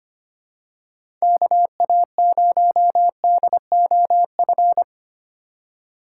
Morse code, heard as a keyed tone, about 25 wpm, 700 Hz. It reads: KA0DOF
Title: USA Call Signs - 3x Character Spacing - 25wpm